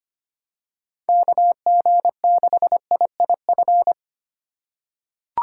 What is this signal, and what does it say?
Morse code, heard as a keyed tone, about 25 wpm, 700 Hz. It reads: KG6IIF
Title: USA Call Signs - 6 Characters - 4x Character Spacing - 25wpm